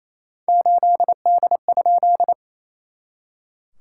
Morse code, heard as a keyed tone, about 28 wpm, 690 Hz. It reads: 8D?
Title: SOTA QSO Practice 28wpm